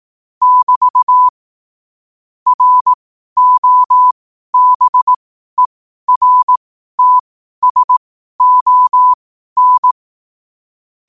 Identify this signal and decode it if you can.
Morse code, heard as a keyed tone, about 18 wpm, spaced as for 12 wpm, 1000 Hz. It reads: <BT> ROBERTSON
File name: qotdfiles / 12wpm